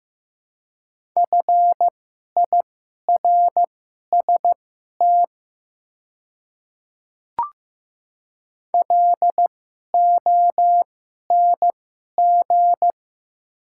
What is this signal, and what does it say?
Morse code, heard as a keyed tone, about 15 wpm, spaced as for 11 wpm, 700 Hz. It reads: FIRST LONG
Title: 2x Character Spacing - Top 100 Words - 15wpm